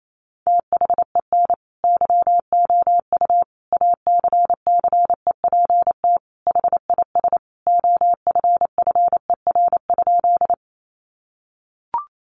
Morse code, heard as a keyed tone, about 28 wpm, 700 Hz. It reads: THEN YOU ACCEPT HIS OFFER?
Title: Sentences from Top 1300 Words 28wpm